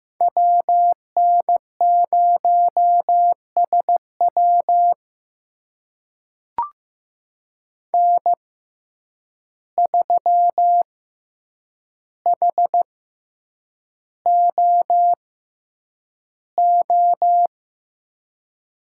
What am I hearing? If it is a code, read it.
Morse code, heard as a keyed tone, about 15 wpm, spaced as for 8 wpm, 700 Hz. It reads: WN0SW N3HOO
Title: USA Call Signs - 5 Characters - 6x Character Spacing - 15wpm